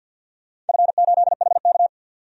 Morse code, heard as a keyed tone, about 52 wpm, 700 Hz. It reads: F8HK
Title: DXCC-100 Call Signs - 4 Characters - 50wpm